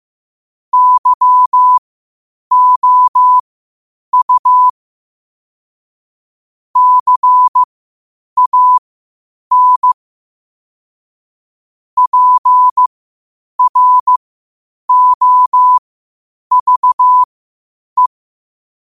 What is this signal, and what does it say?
Morse code, heard as a keyed tone, about 15 wpm, spaced as for 8 wpm, 1000 Hz. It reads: YOU CAN PROVE
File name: qotdfiles / 8wpm